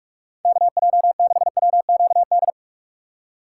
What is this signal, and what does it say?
Morse code, heard as a keyed tone, about 43 wpm, 700 Hz. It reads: KJ6WQD
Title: USA Call Signs - 2x Character Spacing - 45wpm